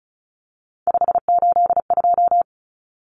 Morse code, heard as a keyed tone, about 34 wpm, 700 Hz. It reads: 582
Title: Numbers - 3-Digit 35wpm